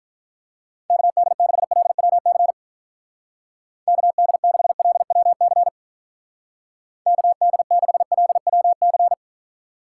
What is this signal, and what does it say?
Morse code, heard as a keyed tone, about 43 wpm, 700 Hz. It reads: KD6LWC KD6LWC KD6LWC
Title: Speed Racing - USA 2x3 Call Signs 35wpm